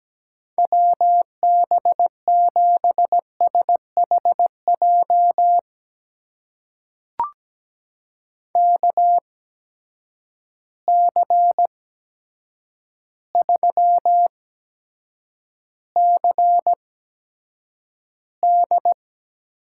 Morse code, heard as a keyed tone, about 17 wpm, spaced as for 8 wpm, 700 Hz. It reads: WB7SHJ KC3CD
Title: USA Call Signs - 6 Characters - 8x Character Spacing - 17wpm